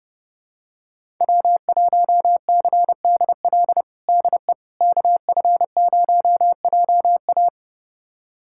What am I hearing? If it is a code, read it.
Morse code, heard as a keyed tone, about 30 wpm, 700 Hz. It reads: W1CDL DE KF0JA
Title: USA Callsign DE Callsign - 5 Characters - 30wpm